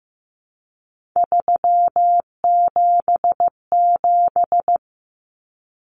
Morse code, heard as a keyed tone, about 15 wpm, 700 Hz. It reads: 377
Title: Common Cut Numbers - 3-Digit 15wpm